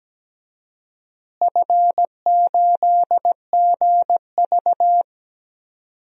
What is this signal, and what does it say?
Morse code, heard as a keyed tone, about 17 wpm, 700 Hz. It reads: F8GV
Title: DXCC-100 Call Signs - 4 Characters - Repeated 2x - 17wpm